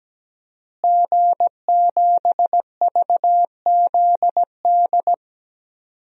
Morse code, heard as a keyed tone, about 17 wpm, 700 Hz. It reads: G7VZD
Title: DXCC-100 Call Signs - 5 Characters - Repeated 3x - 17wpm